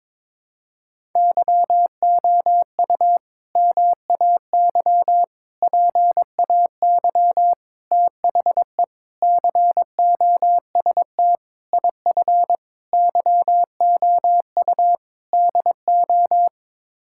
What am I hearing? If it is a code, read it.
Morse code, heard as a keyed tone, about 22 wpm, 700 Hz. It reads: YOU MAY PAY THE COST IF YOU DO
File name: practice / 22wpm / Sets of 8 Words - 22wpm